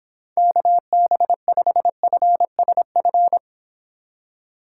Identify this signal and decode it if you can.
Morse code, heard as a keyed tone, about 26 wpm, 700 Hz. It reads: KB5FSF